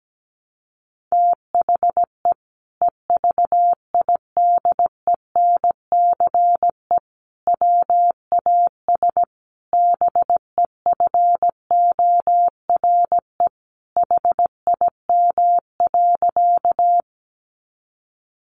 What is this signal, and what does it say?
Morse code, heard as a keyed tone, about 17 wpm, 700 Hz. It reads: THE EVIDENCE WAS BEFORE HIM.
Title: Sentences from Top 1300 Words 17wpm